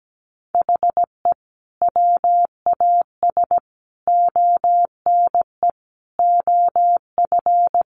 Morse code, heard as a keyed tone, about 17 wpm, 700 Hz. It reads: HE WAS ONE OF
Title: Sets of 5 Words 17wpm